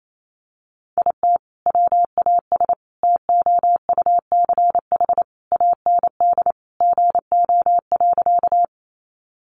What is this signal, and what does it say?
Morse code, heard as a keyed tone, about 28 wpm, 700 Hz. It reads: IT WAS TOUCH AND GO.
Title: Sentences from Top 1200 Words 28wpm